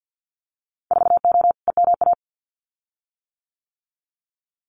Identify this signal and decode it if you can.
Morse code, heard as a keyed tone, about 48 wpm, 700 Hz. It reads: 4O EMA
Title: Winter Field Day Exchanges 50wpm